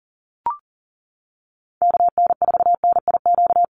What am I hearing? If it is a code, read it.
Morse code, heard as a keyed tone, about 40 wpm, 700 Hz. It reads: KN4NIQ
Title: USA Call Signs Repeated 3x 40wpm